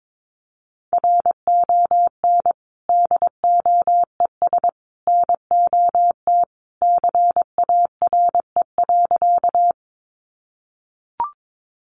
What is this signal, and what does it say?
Morse code, heard as a keyed tone, about 22 wpm, 700 Hz. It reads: RON DOES NOT CARE.
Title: Additional Easy Phrases 22wpm